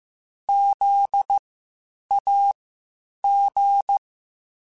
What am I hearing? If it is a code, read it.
Morse code, heard as a keyed tone, about 15 wpm, spaced as for 8 wpm, 800 Hz. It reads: ZAG
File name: Spaced Alphas